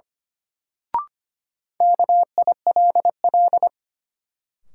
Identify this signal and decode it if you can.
Morse code, heard as a keyed tone, about 25 wpm, 700 Hz. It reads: KILL
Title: Sentences from Top 1400 Words - New Word Drill - 25wpm